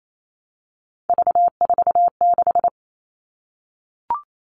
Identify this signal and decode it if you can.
Morse code, heard as a keyed tone, about 28 wpm, 700 Hz. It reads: V46
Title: All Cut Numbers - 3-Digit 28wpm